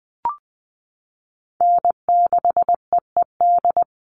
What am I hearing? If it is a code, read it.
Morse code, heard as a keyed tone, about 20 wpm, 700 Hz. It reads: N6EED
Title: Call Signs - 20wpm